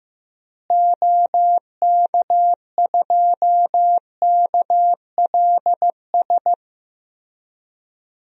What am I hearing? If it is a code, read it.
Morse code, heard as a keyed tone, about 15 wpm, 700 Hz. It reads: OK2KLS
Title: DXCC-100 Call Signs - 6 Characters - Repeated 3x - 15wpm